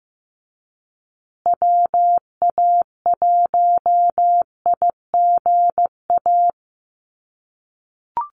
Morse code, heard as a keyed tone, about 15 wpm, 700 Hz. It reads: WA1IGA
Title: USA Call Signs Repeated 3x 15wpm